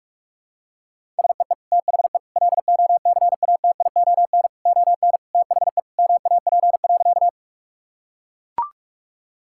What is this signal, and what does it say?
Morse code, heard as a keyed tone, about 42 wpm, 700 Hz. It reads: SEE THE LOCATION ON THE MAP.
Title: Sentences from Top 1000 Words - Review Part 6 - 45wpm